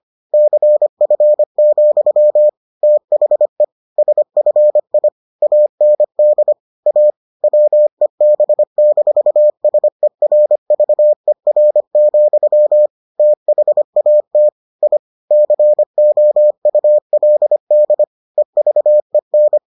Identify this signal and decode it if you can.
Morse code, heard as a keyed tone, about 25 wpm, 600 Hz. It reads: CF, THE SFI AND A WEB-SERVER, THAT I COULD EVEN